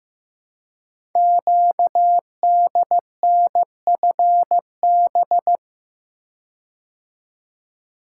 Morse code, heard as a keyed tone, about 15 wpm, 700 Hz. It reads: QDNFB
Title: Code Groups - 5 Letters - 15wpm